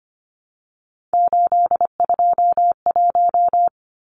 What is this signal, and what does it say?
Morse code, heard as a keyed tone, about 25 wpm, 700 Hz. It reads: 821
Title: Numbers - 3-Digit 25wpm